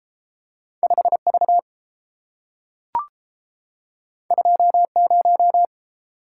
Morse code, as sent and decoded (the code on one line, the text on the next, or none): ..... / ...- / ..--- / -----
5 V 2 0